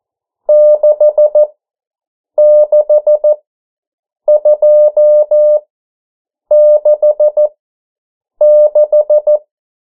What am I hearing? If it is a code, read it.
Morse code, heard as a keyed tone, about 14 wpm, spaced as for 7 wpm, 600 Hz. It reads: 66266
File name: morse mp3 / 09 urok 8